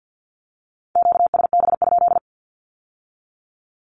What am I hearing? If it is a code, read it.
Morse code, heard as a keyed tone, about 50 wpm, 700 Hz. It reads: QSB?
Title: Q-Codes with Spoken Meaning - 50wpm